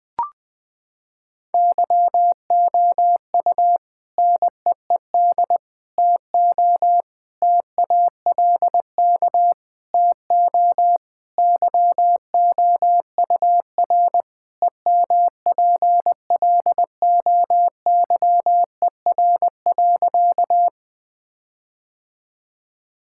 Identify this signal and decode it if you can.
Morse code, heard as a keyed tone, about 21 wpm, 700 Hz. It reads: YOU NEED TO TALK TO YOUR EMPLOYER.
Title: Sentences from Top 1600 Words 20wpm